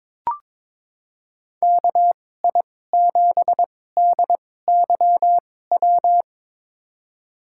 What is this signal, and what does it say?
Morse code, heard as a keyed tone, about 22 wpm, spaced as for 16 wpm, 700 Hz. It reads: KI7DYW
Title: USA Call Signs - 6 Characters - 2x Character Spacing - 22wpm